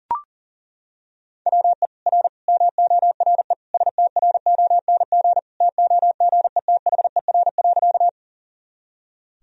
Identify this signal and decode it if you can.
Morse code, heard as a keyed tone, about 37 wpm, 700 Hz. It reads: WE R MORE STRONG TOGETHER.
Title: Sentences from Top 500 Words and Abbreviations 40wpm